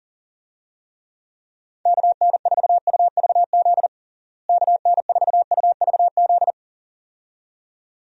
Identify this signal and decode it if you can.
Morse code, heard as a keyed tone, about 37 wpm, 700 Hz. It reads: KN4UVZ KN4UVZ
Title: USA Call Signs - 6 Characters - Repeated 2x - 40wpm